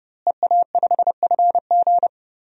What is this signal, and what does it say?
Morse code, heard as a keyed tone, about 30 wpm, 700 Hz. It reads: EA5FG
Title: DXCC-100 Call Signs - 5 Characters - Repeated 2x - 30wpm